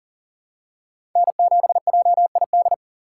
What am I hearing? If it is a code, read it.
Morse code, heard as a keyed tone, about 40 wpm, 700 Hz. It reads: N7JID
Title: USA Call Signs - 6x Character Spacing - 40wpm